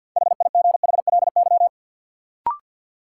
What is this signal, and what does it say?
Morse code, heard as a keyed tone, about 53 wpm, 700 Hz. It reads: HIGHLY
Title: Sentences from Top 1100 Words - New Word Drill Alt - 50wpm